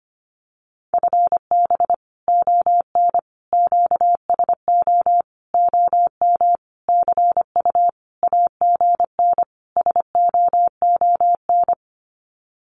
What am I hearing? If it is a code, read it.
Morse code, heard as a keyed tone, about 25 wpm, 700 Hz. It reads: FB ON QSO OM CU AGN SOON
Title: Common QSO Phrases 25wpm